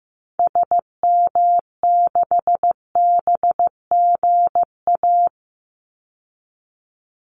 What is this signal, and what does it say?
Morse code, heard as a keyed tone, about 15 wpm, 700 Hz. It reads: SM6BGA
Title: Call Signs Repeated 2x - 15wpm